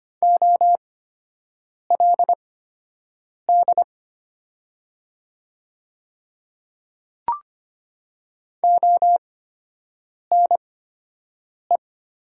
Morse code, heard as a keyed tone, about 25 wpm, spaced as for 7 wpm, 700 Hz. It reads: OLD ONE
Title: 8x Character Spacing - Top 200 Words - 25wpm